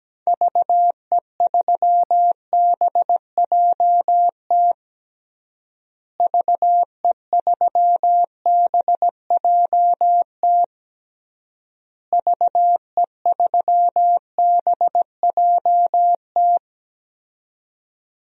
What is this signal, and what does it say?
Morse code, heard as a keyed tone, about 17 wpm, 700 Hz. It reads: VE3BJT VE3BJT VE3BJT
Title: DXCC-100 Call Signs - 6 Characters - Repeated 3x - 17wpm